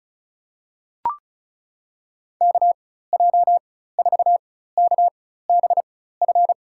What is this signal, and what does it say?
Morse code, heard as a keyed tone, about 36 wpm, spaced as for 16 wpm, 700 Hz. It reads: KJ4KBF
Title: USA Call Signs - 4x Character Spacing - 35wpm